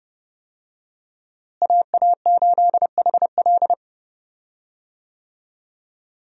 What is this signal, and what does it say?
Morse code, heard as a keyed tone, about 29 wpm, 700 Hz. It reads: AA8HL
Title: USA Call Signs - 5 Characters - Repeated 3x - 30wpm